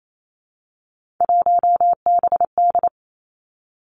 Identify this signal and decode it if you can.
Morse code, heard as a keyed tone, about 28 wpm, 700 Hz. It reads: 1BD
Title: Three Challenging Letter-Number Combinations 28wpm